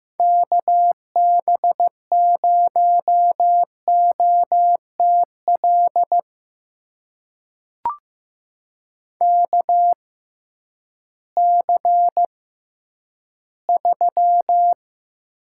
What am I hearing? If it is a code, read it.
Morse code, heard as a keyed tone, about 15 wpm, spaced as for 10 wpm, 700 Hz. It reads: KB0OTL KC3
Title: USA Call Signs - 6 Characters - 6x Character Spacing - 15wpm